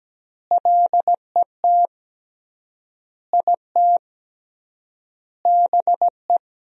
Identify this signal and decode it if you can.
Morse code, heard as a keyed tone, about 17 wpm, 700 Hz. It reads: LET IT BE
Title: 3x Word-Spacing - Sentences from Top 500 Words - 17wpm